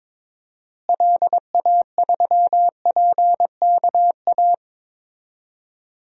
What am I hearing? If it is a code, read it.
Morse code, heard as a keyed tone, about 21 wpm, 700 Hz. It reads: LA3PKA
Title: DXCC-100 Call Signs - 6 Characters - Repeated 2x - 22wpm